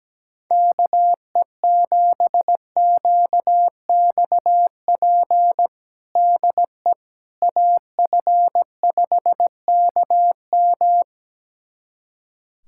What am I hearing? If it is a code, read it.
Morse code, heard as a keyed tone, about 17 wpm, 700 Hz. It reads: KE7QXP DE AF5KM
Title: USA Callsign DE Callsign 17wpm